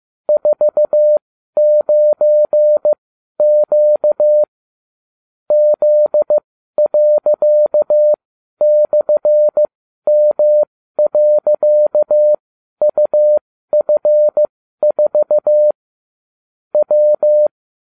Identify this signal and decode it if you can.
Morse code, heard as a keyed tone, about 15 wpm, spaced as for 12 wpm, 600 Hz. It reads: 49Q Z./M.UF4 W